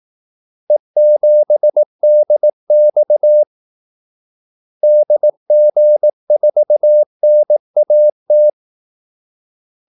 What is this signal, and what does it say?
Morse code, heard as a keyed tone, about 18 wpm, 600 Hz. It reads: E7DX DG4NAT
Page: Kaldesignaler 18 wpm